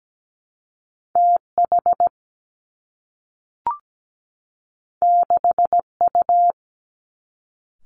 Morse code, heard as a keyed tone, about 17 wpm, 700 Hz. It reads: TH 6U
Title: Two Challenging Letter-Number Combinations 17wpm